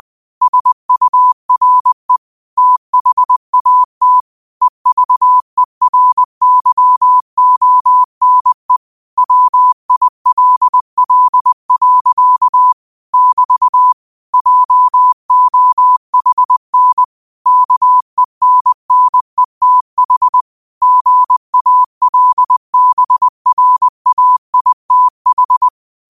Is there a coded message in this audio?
Yes: Morse code from 0.4 to 25.7 s, heard as a keyed tone, about 20 wpm, 1000 Hz. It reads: SURE THAT EVERYONE WILL. <BT> JOHN KENNETH GALBRAITH